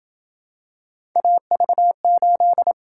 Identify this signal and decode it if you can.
Morse code, heard as a keyed tone, about 27 wpm, 700 Hz. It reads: AV8